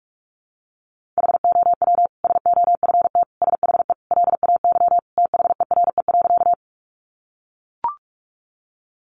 Morse code, heard as a keyed tone, about 44 wpm, 700 Hz. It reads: HOW SOFT SHE LAY THERE.
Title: Sentences from Top 1500 Words 45wpm